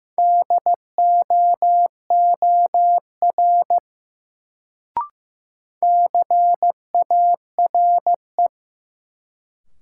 Morse code, heard as a keyed tone, about 15 wpm, 700 Hz. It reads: DOOR CARE
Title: Sentences from Top 300 Words - New Word Drill - 15wpm